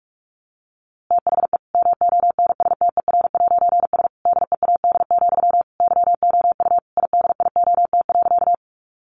Morse code, heard as a keyed tone, about 46 wpm, 700 Hz. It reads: THE MONSTER'S DEAD, YOU IDIOT.
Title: A Sound Of Thunder 45wpm